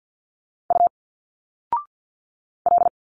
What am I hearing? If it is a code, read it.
Morse code, heard as a keyed tone, about 50 wpm, 700 Hz. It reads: U L
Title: Single Letters 50wpm